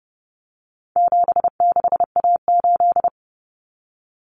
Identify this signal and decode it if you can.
Morse code, heard as a keyed tone, about 30 wpm, 700 Hz. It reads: 76A8
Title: Common Cut Numbers - 4-Digit 30wpm